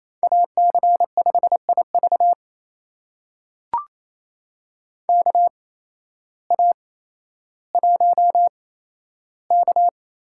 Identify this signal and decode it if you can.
Morse code, heard as a keyed tone, about 28 wpm, spaced as for 13 wpm, 700 Hz. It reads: AC5IV KA1K
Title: USA Call Signs - 8x Character Spacing - 28wpm